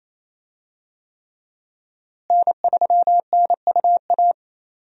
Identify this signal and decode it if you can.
Morse code, heard as a keyed tone, about 28 wpm, 700 Hz. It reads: N3NUA
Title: USA Call Signs - 5 Characters - Repeated 3x - 28wpm